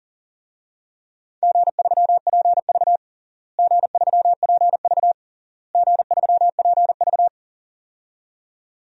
Morse code, heard as a keyed tone, about 40 wpm, 700 Hz. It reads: G3PV G3PV G3PV
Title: DXCC-100 Call Signs - 4 Characters - Repeated 3x - 40wpm